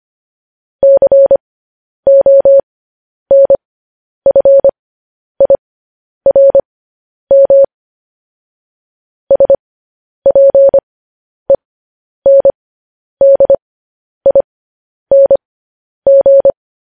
Morse code, heard as a keyed tone, about 25 wpm, spaced as for 10 wpm, 550 Hz. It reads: CONFIRM SPENDING
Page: news-headlines-in-morse-code-at-10-wpm